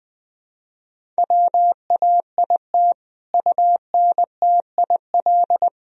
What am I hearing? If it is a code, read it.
Morse code, heard as a keyed tone, about 20 wpm, 700 Hz. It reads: WAIT UNTIL
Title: Sentences from Top 1000 Words - Review Part 1 - 20wpm